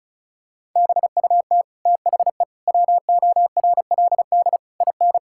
Morse code, heard as a keyed tone, about 36 wpm, 700 Hz. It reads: BUT THE WORLD IN
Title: Sentences from Top 1000 Words - Review Part 6 - 35wpm